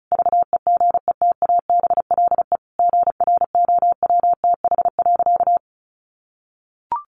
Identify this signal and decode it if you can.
Morse code, heard as a keyed tone, about 34 wpm, 700 Hz. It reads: VEGETABLE GROWTH.
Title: Sentences from Top 1300 Words 35wpm